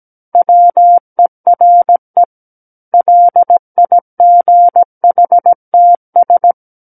Morse code, heard as a keyed tone, about 17 wpm, 700 Hz. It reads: WERE LIGHTS
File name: Book One / 08-Chapter 8 - 17 WPM